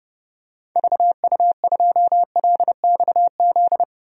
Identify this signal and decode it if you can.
Morse code, heard as a keyed tone, about 31 wpm, 700 Hz. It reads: VU2LXZ
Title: DXCC-100 Call Signs - 6 Characters - 30wpm